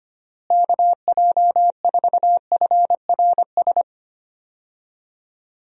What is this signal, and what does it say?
Morse code, heard as a keyed tone, about 25 wpm, 700 Hz. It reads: KJ4FRS
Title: USA Call Signs - 6 Characters - Repeated 3x - 25wpm